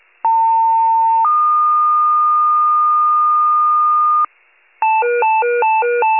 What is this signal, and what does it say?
Two-tone page, heard as a one-second tone, then a three-second tone, about 900 then 1220 Hz. It is heard from 0.2 to 4.3 s.